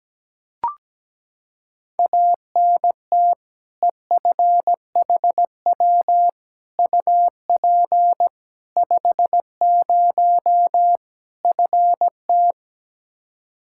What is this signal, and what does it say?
Morse code, heard as a keyed tone, about 17 wpm, 700 Hz. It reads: ANT EFHW UP 50 FT
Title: Practice QSO - Exchange 2 - Info - 17wpm